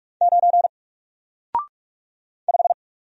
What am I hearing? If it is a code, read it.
Morse code, heard as a keyed tone, about 44 wpm, 700 Hz. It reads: 9 5
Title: Single Letter-Number 45wpm